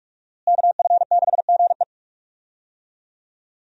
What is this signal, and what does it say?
Morse code, heard as a keyed tone, about 46 wpm, 700 Hz. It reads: KF6GE
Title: USA Call Signs - 5 Characters - Repeated 2x - 45wpm